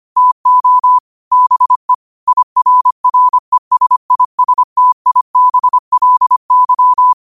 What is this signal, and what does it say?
Morse code, heard as a keyed tone, about 25 wpm, 1000 Hz. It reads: TO BE IRRESISTIBLY
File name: qotdfiles / 25wpm